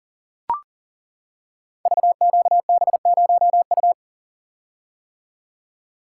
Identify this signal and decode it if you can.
Morse code, heard as a keyed tone, about 38 wpm, 690 Hz. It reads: VQB0U
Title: Code Groups - 5 Letter:Numbers - 40wpm